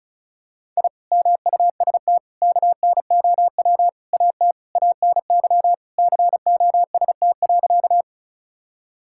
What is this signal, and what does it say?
Morse code, heard as a keyed tone, about 36 wpm, 700 Hz. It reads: I MUST KNOW AT ANY COST.
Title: Sentences from Top 1000 Words - Review Part 5 - 35wpm